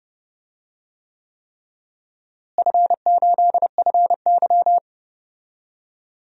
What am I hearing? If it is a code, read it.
Morse code, heard as a keyed tone, about 32 wpm, 700 Hz. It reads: F8FY
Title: DXCC-100 Call Signs - 4 Characters - Repeated 3x - 30wpm